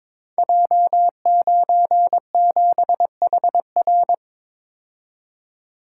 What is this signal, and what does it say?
Morse code, heard as a keyed tone, about 22 wpm, 700 Hz. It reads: J97HR
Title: Code Groups - 5 Letter:Numbers - 22wpm